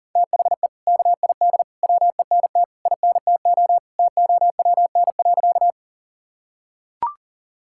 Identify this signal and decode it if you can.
Morse code, heard as a keyed tone, about 40 wpm, 700 Hz. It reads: THE KID WENT INTO TOWN.
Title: Sentences from Top 1600 Words 40wpm